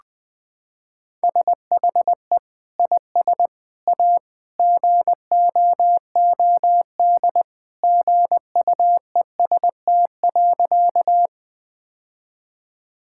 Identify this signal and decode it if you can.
Morse code, heard as a keyed tone, about 21 wpm, 700 Hz. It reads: SHE IS A GOOD GUEST.
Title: Sentences from Top 1600 Words 20wpm